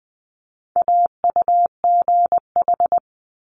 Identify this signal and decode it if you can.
Morse code, heard as a keyed tone, about 19 wpm, 700 Hz. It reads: AUGH
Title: Multi-letter Phonemes Quick Fire 20wpm